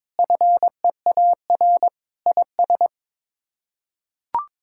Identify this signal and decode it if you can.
Morse code, heard as a keyed tone, about 22 wpm, 700 Hz. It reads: FEAR IS
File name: practice / 22wpm / Sets of 8 Words - 22wpm